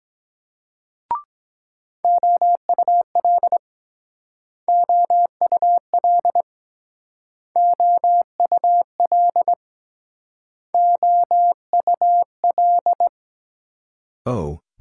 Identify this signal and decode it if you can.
Morse code, heard as a keyed tone, about 23 wpm, 700 Hz. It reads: OUL OUL OUL OUL